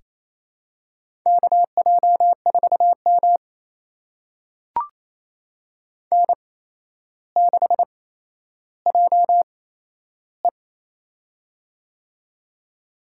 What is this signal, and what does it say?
Morse code, heard as a keyed tone, about 27 wpm, spaced as for 12 wpm, 700 Hz. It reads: KJ4M N6JE
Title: USA Call Signs - 4 Characters - 8x Character Spacing - 28wpm